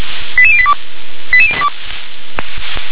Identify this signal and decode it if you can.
Five-tone selective call (ZVEI), heard as quick runs of five tones, tones of about 70 ms. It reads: 88082
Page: SDR/ZVEI